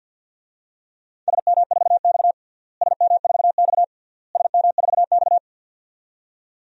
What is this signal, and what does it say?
Morse code, heard as a keyed tone, about 49 wpm, 700 Hz. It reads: SM4X SM4X SM4X
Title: DXCC-100 Call Signs - 4 Characters - Repeated 3x - 50wpm